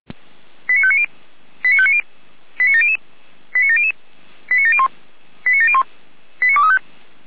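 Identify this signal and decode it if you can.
Five-tone selective call (ZVEI), heard as quick runs of five tones, tones of about 70 ms. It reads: 89590 89590 89700 89700 89701 89701 89235
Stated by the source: ZVEI five-tone selective call received on SDR.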